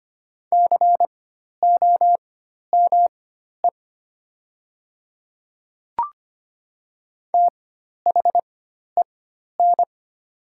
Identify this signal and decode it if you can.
Morse code, heard as a keyed tone, about 25 wpm, spaced as for 12 wpm, 700 Hz. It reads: COME THEN
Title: 4x Character Spacing - Top 200 Words - 25wpm